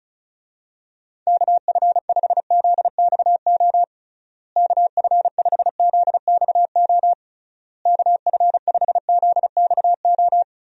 Morse code, heard as a keyed tone, about 36 wpm, 700 Hz. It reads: KF5ZXO KF5ZXO KF5ZXO
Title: USA Call Signs - 6 Characters - Repeated 3x - 35wpm